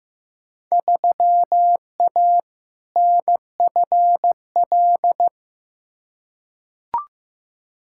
Morse code, heard as a keyed tone, about 15 wpm, 700 Hz. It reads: 3A NFL
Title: Field Day Exchanges - 15wpm